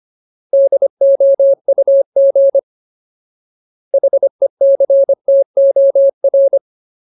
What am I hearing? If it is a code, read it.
Morse code, heard as a keyed tone, about 25 wpm, 550 Hz. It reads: DOUG HECTOR
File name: Ultimate Word List / 25WPM Ultimate